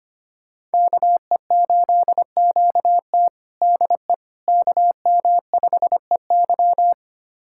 Morse code, heard as a keyed tone, about 25 wpm, 700 Hz. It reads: KE8QT DE KM5EY
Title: USA Callsign DE Callsign - 5 Characters - 25wpm